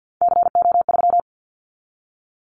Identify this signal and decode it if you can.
Morse code, heard as a keyed tone, about 50 wpm, 700 Hz. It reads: CO3
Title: Callsign Prefixes for Top 100 DXCC - 50wpm